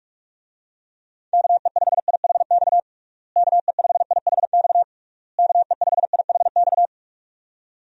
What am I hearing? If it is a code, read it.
Morse code, heard as a keyed tone, about 41 wpm, 700 Hz. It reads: KE5IHX KE5IHX KE5IHX
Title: USA Call Signs - 6 Characters - Repeated 3x - 45wpm